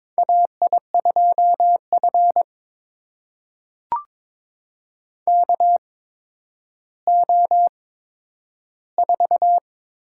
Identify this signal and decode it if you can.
Morse code, heard as a keyed tone, about 22 wpm, spaced as for 11 wpm, 700 Hz. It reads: AI2F KO4
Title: USA Call Signs - 8x Character Spacing - 22wpm